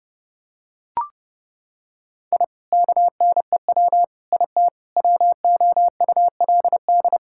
Morse code, heard as a keyed tone, about 30 wpm, 700 Hz. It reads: I KNEW IT WOULD
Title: Sentences from Top 1000 Words - Review Part 6 - 30wpm